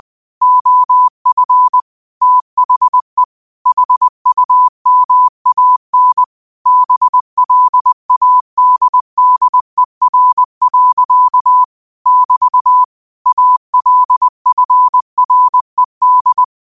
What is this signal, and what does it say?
Morse code, heard as a keyed tone, about 20 wpm, 1000 Hz. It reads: OF THE HUMAN BLADDER. <BT> ALFRED
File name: qotdfiles / 20wpm